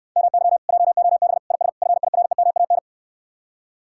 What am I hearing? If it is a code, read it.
Morse code, heard as a keyed tone, about 69 wpm, 700 Hz. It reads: MY JOB IS FIREMAN